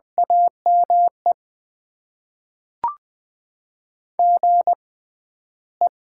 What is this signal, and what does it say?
Morse code, heard as a keyed tone, about 20 wpm, spaced as for 12 wpm, 700 Hz. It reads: AME GE
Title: Top 20 QSO Elements - 6x Character Spacing - 20wpm